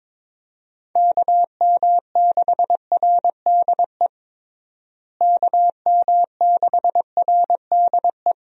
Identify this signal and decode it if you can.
Morse code, heard as a keyed tone, about 22 wpm, 700 Hz. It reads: KM6RDE KM6RDE
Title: USA Call Signs - 6 Characters - Repeated 2x - 22wpm